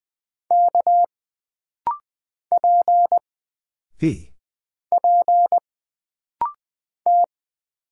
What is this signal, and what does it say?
Morse code, heard as a keyed tone, about 20 wpm, 700 Hz. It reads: K P P T
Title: Single Character (Letter:Number:Punctuation:Prosign) - Rapid-Fire - 20wpm